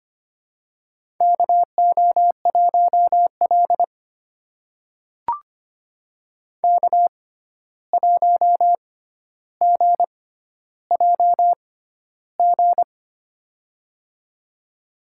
Morse code, heard as a keyed tone, about 25 wpm, spaced as for 12 wpm, 700 Hz. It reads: KO1L K1GJG
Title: USA Call Signs - 6x Character Spacing - 25wpm